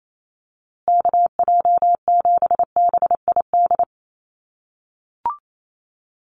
Morse code, heard as a keyed tone, about 28 wpm, 700 Hz. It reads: KJ7BID
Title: USA Call Signs Repeated 3x 28wpm